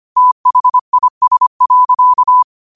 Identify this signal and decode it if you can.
Morse code, heard as a keyed tone, about 25 wpm, 1000 Hz. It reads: THIS.